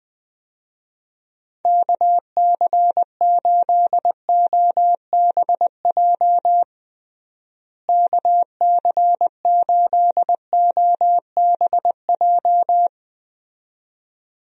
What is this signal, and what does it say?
Morse code, heard as a keyed tone, about 20 wpm, 700 Hz. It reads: KC8OBJ KC8OBJ
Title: USA Call Signs - 6 Characters - Repeated 2x - 20wpm